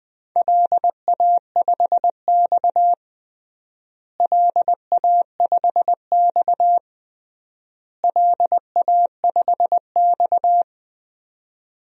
Morse code, heard as a keyed tone, about 20 wpm, 700 Hz. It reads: LA5X LA5X LA5X
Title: DXCC-100 Call Signs - 4 Characters - Repeated 3x - 20wpm